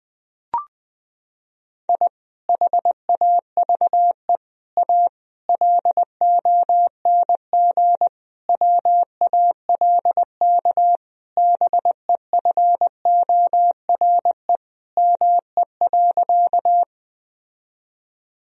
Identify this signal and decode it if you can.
Morse code, heard as a keyed tone, about 21 wpm, 700 Hz. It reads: I HAVE A LONG WALK BEFORE ME.
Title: Sentences from Top 1000 Words - Review Part 1 - 20wpm